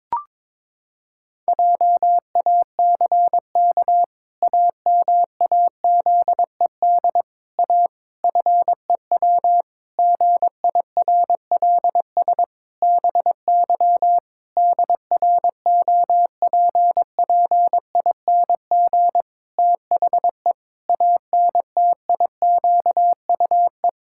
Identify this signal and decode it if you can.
Morse code, heard as a keyed tone, about 22 wpm, 700 Hz. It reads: JACK AMAZED A FEW GIRLS BY DROPPING THE ANTIQUE
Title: Pangram Challenge v2 - 22wpm